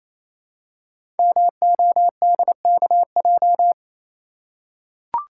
Morse code, heard as a keyed tone, about 27 wpm, 700 Hz. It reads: MODKJ
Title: Code Groups - 5 Letter:Numbers - 28wpm